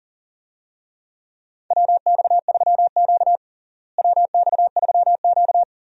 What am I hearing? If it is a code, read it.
Morse code, heard as a keyed tone, about 37 wpm, 700 Hz. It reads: WX3Q WX3Q
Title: USA Call Signs - 4 Characters - Repeated 2x - 40wpm